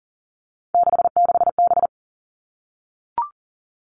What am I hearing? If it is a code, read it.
Morse code, heard as a keyed tone, about 40 wpm, 700 Hz. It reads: 66B
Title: All Cut Numbers - 3-Digit 40wpm